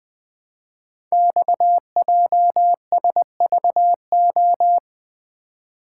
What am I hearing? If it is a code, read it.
Morse code, heard as a keyed tone, about 20 wpm, 700 Hz. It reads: XJSVO
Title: Code Groups - 5 Letters - 20wpm